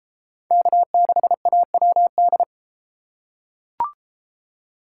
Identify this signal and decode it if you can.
Morse code, heard as a keyed tone, about 33 wpm, 700 Hz. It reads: K6AWD